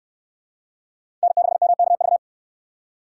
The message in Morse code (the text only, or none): N6MKV